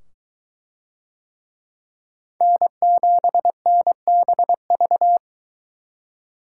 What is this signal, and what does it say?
Morse code, heard as a keyed tone, about 22 wpm, 700 Hz. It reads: N7NBV